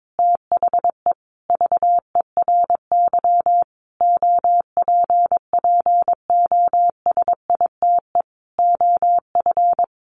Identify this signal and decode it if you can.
Morse code, heard as a keyed tone, about 22 wpm, 700 Hz. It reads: THE VERY OPPOSITE OF
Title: Sentences from Top 1300 Words 22wpm